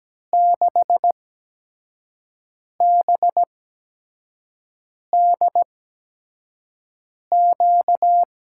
Morse code, heard as a keyed tone, about 17 wpm, spaced as for 5 wpm, 700 Hz. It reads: 6BDQ
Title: USA Call Signs - 6 Characters - 8x Character Spacing - 17wpm